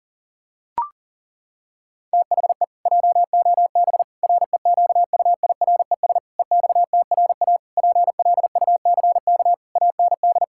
Morse code, heard as a keyed tone, about 37 wpm, 700 Hz. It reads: THE JOB REQUIRES EXTRA PLUCK AND
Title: Pangram Challenge v2 - 40wpm